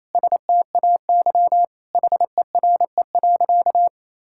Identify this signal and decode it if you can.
Morse code, heard as a keyed tone, about 28 wpm, 700 Hz. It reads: STAY HERE.
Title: Sentences from Top 1600 Words 28wpm